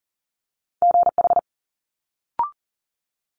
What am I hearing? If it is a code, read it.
Morse code, heard as a keyed tone, about 39 wpm, 700 Hz. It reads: GH
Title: Multi-letter Phonemes Quick Fire 40wpm